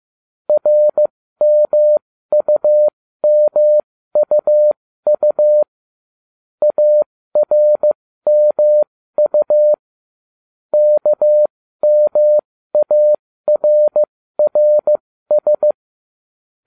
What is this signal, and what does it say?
Morse code, heard as a keyed tone, about 15 wpm, spaced as for 13 wpm, 600 Hz. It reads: RMUMUU ARMU KMARRS
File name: ubacw05 2